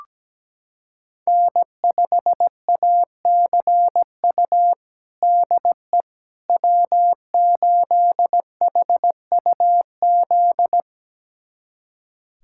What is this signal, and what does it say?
Morse code, heard as a keyed tone, about 17 wpm, 700 Hz. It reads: N5ACU DE W8HUZ
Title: USA Callsign DE Callsign - 5 Characters - 17wpm